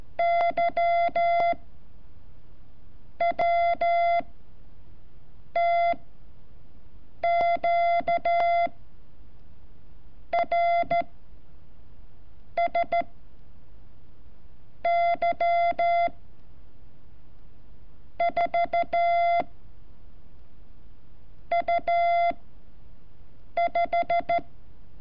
Morse code, heard as a keyed tone, about 14 wpm, 680 Hz. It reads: Y W T Q R S Y 4 U 5
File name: curso / ejercicio 28